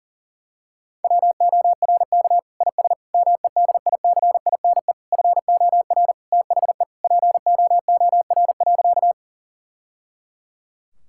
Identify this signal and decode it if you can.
Morse code, heard as a keyed tone, about 40 wpm, 700 Hz. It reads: WORK IS MEDICINE FOR THE POOR.
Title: Sentences from Top 1000 Words - Review Part 6 - 40wpm